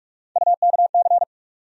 Morse code, heard as a keyed tone, about 41 wpm, 700 Hz. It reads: UKC
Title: USA Call Sign Postfixes - 3 Letters 45wpm